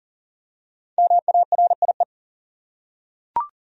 Morse code, heard as a keyed tone, about 37 wpm, 700 Hz. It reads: MARIE
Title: USA Common Names - 40wpm